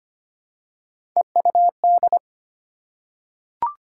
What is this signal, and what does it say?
Morse code, heard as a keyed tone, about 25 wpm, 700 Hz. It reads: EUD
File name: Three Letter-Number Combinations - 25wpm